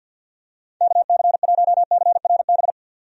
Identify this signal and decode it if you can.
Morse code, heard as a keyed tone, about 46 wpm, 700 Hz. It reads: KC1CRB
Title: USA Call Signs - 6 Characters - 50wpm